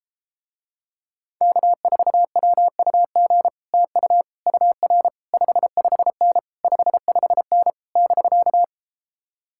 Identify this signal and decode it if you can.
Morse code, heard as a keyed tone, about 33 wpm, 700 Hz. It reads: K4WUG TU UR 55N 55N <BK>